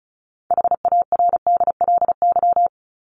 Morse code, heard as a keyed tone, about 34 wpm, 700 Hz. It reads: HARDLY
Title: LY Suffix - Top 100 Words with Base Words 35wpm